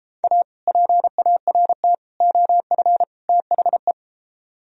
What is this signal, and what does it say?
Morse code, heard as a keyed tone, about 32 wpm, 700 Hz. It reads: A PART OF THE